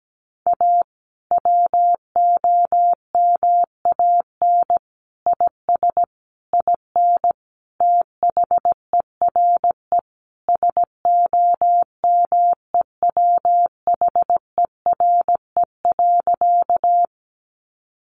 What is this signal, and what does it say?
Morse code, heard as a keyed tone, about 17 wpm, 700 Hz. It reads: A WOMAN IS IN THERE SOMEWHERE.
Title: Sentences from Top 1400 Words 17wpm